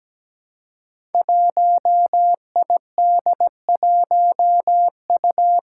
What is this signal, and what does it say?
Morse code, heard as a keyed tone, about 17 wpm, 700 Hz. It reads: 1ID1U
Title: Five Challenging Letter-Number Combinations 17wpm